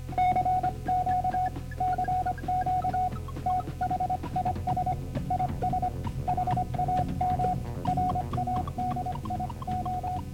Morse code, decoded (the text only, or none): CQ CQ THIS IS HAM RADIO